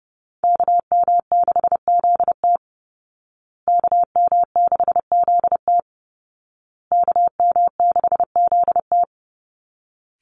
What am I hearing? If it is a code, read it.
Morse code, heard as a keyed tone, about 28 wpm, 700 Hz. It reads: KM6ZT KM6ZT KM6ZT
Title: Call Signs Repeated 2x - 30wpm